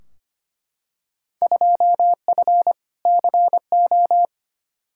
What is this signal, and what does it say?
Morse code, heard as a keyed tone, about 25 wpm, 700 Hz. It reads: 2F CO
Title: Field Day Exchanges - 25wpm